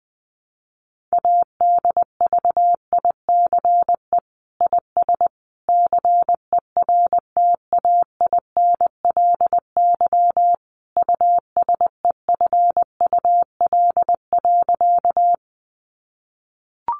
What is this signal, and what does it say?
Morse code, heard as a keyed tone, about 20 wpm, 700 Hz. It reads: ADVICE IS CERTAINLY USEFUL.
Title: Sentences from Top 1100 Words 20wpm